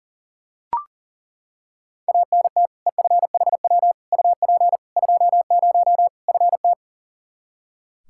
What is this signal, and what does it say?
Morse code, heard as a keyed tone, about 37 wpm, 700 Hz. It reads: ANT EFHW UP 20 FT
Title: Practice QSO - Exchange 2 - Info - 40wpm